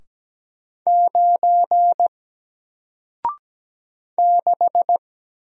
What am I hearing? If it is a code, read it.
Morse code, heard as a keyed tone, about 17 wpm, 700 Hz. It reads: 9 6
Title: Single Letter-Number Alt 17wpm